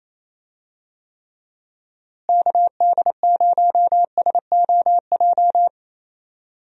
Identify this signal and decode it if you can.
Morse code, heard as a keyed tone, about 27 wpm, 700 Hz. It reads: KD0SOJ
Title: USA Call Signs - 6 Characters - Repeated 2x - 28wpm